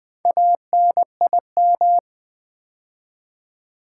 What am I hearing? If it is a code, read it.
Morse code, heard as a keyed tone, about 20 wpm, 700 Hz. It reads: ANIM
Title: Build A Word - 2 Letters At A Time - 6-Letter Words - 20wpm